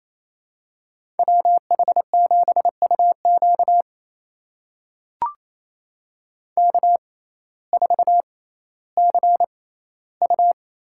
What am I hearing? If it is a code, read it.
Morse code, heard as a keyed tone, about 27 wpm, spaced as for 15 wpm, 700 Hz. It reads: WH7UQ K4CU
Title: USA Call Signs - 6x Character Spacing - 28wpm